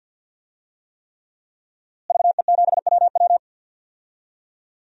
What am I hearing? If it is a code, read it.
Morse code, heard as a keyed tone, about 52 wpm, 700 Hz. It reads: VE7WW
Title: DXCC-100 Call Signs - 5 Characters - Repeated 3x - 50wpm